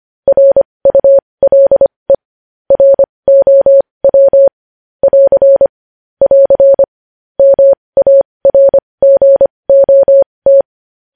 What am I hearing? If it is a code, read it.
Morse code, heard as a keyed tone, about 25 wpm, spaced as for 20 wpm, 550 Hz. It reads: RULE ROW <AR> <AR> MARGOT